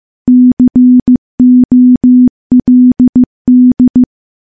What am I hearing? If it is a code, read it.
Morse code, heard as a keyed tone, about 15 wpm, 250 Hz. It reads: COLD
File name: morse-training-15WPM